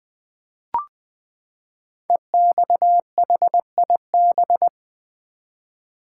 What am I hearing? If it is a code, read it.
Morse code, heard as a keyed tone, about 20 wpm, 700 Hz. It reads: EXHIB
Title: Build A Word - 5 Letters At A Time - 10-Letter Words - 20wpm